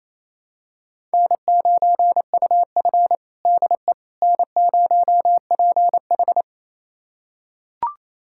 Morse code, heard as a keyed tone, about 27 wpm, 700 Hz. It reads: N9UF DE N0PH
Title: USA Callsign DE Callsign - 4 Characters - 28wpm